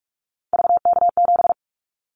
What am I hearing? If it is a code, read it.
Morse code, heard as a keyed tone, about 44 wpm, 700 Hz. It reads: VK7
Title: Callsign Prefixes for Top 100 DXCC - 45wpm